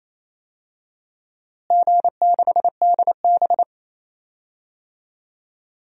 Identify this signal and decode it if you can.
Morse code, heard as a keyed tone, about 28 wpm, 700 Hz. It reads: G6DB
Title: DXCC-100 Call Signs - 4 Characters - Repeated 2x - 28wpm